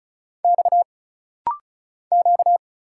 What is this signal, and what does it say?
Morse code, heard as a keyed tone, about 34 wpm, 700 Hz. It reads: X Q
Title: Single Letters - Rapid-Fire 35wpm